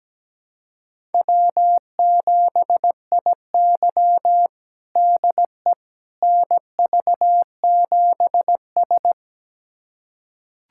Morse code, heard as a keyed tone, about 17 wpm, 700 Hz. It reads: W7IY DE NV7S
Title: USA Callsign DE Callsign - 4 Characters - 17wpm